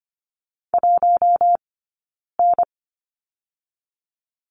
Morse code, heard as a keyed tone, about 25 wpm, 700 Hz. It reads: 1 N
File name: course / 25:10wpm / 092 Lesson 16d - Letter c - Two Characters 25wpm